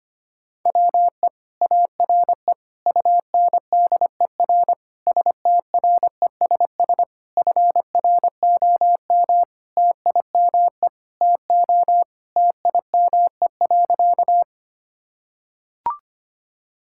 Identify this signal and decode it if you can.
Morse code, heard as a keyed tone, about 25 wpm, 700 Hz. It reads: WE ARE UNDER STRESS FROM TIME TO TIME.
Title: Sentences from Top 1000 Words - Review Part 6 - 25wpm